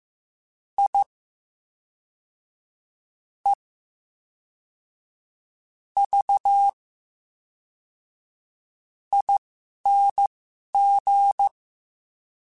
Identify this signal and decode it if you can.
Morse code, heard as a keyed tone, about 15 wpm, spaced as for 11 wpm, 800 Hz. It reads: I E V ING
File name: Special INGs / Special INGs-mp3s / VerySlowToFaster Ing @10